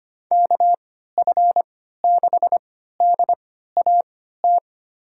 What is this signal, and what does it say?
Morse code, heard as a keyed tone, about 25 wpm, spaced as for 14 wpm, 700 Hz. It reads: KF6DAT
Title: USA Call Signs - 3x Character Spacing - 25wpm